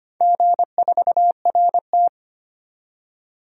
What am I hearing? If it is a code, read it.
Morse code, heard as a keyed tone, about 25 wpm, 700 Hz. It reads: G4RT
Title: DXCC-100 Call Signs - 4 Characters - 25wpm